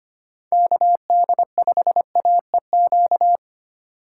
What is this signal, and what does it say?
Morse code, heard as a keyed tone, about 32 wpm, 700 Hz. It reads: KD5AEQ